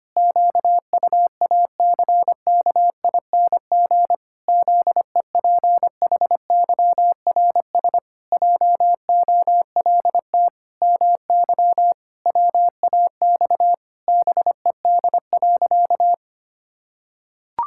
Morse code, heard as a keyed tone, about 25 wpm, 700 Hz. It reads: QUACKING ZEPHYRS JOLT MY WAX BED.
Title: Pangram Challenge v2 - 25wpm